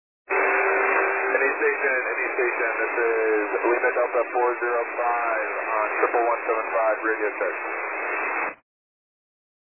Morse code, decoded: none

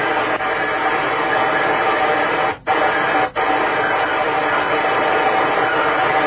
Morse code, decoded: none

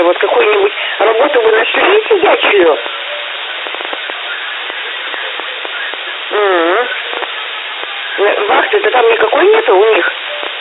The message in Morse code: none

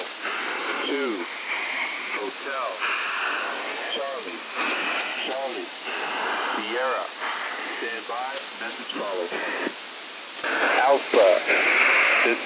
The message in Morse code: none

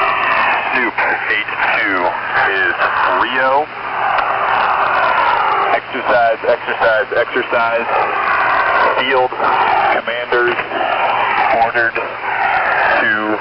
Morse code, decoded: none

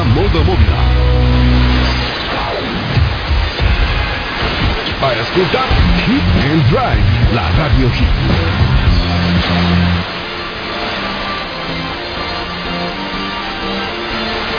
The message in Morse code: none

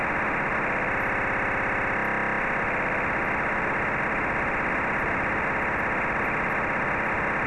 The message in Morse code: none

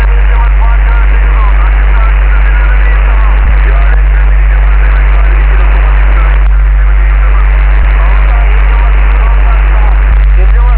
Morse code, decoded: none

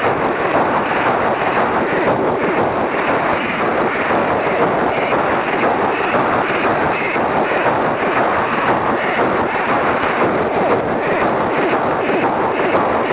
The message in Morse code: none

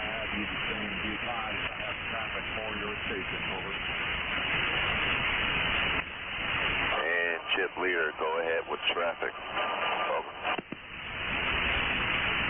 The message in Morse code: none

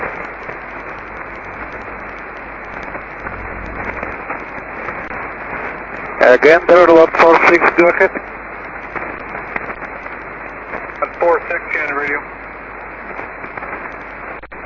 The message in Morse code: none

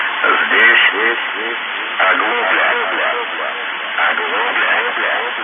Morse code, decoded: none

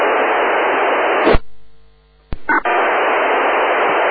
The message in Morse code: none

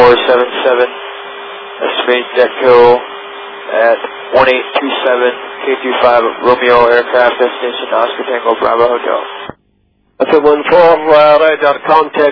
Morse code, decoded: none

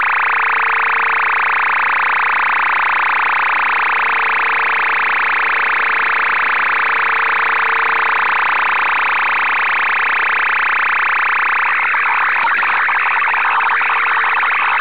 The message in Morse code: none

